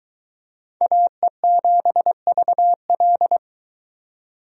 Morse code, heard as a keyed tone, about 22 wpm, 700 Hz. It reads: AE7VL